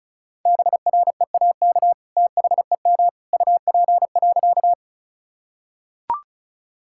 Morse code, heard as a keyed tone, about 34 wpm, 700 Hz. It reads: BREAK THEM UP.
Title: Sentences from Top 1000 Words - Review Part 6 - 35wpm